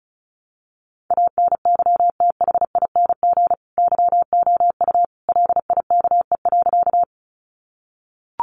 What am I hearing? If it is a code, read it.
Morse code, heard as a keyed tone, about 34 wpm, 700 Hz. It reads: ANYTHING YOU LIKE.
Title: Sentences from Top 1000 Words 35wpm